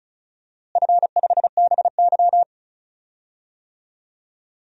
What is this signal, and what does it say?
Morse code, heard as a keyed tone, about 36 wpm, 700 Hz. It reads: F5BY
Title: DXCC-100 Call Signs - 4 Characters - Repeated 3x - 35wpm